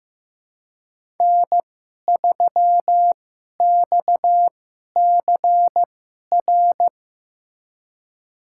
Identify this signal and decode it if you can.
Morse code, heard as a keyed tone, about 15 wpm, spaced as for 11 wpm, 700 Hz. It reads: N3XCR
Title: USA Call Signs - 5 Characters - 2x Character Spacing - 15wpm